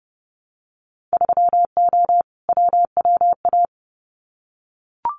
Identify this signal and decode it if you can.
Morse code, heard as a keyed tone, about 30 wpm, 700 Hz. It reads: 3O WWA
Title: Winter Field Day Exchanges Encore 30wpm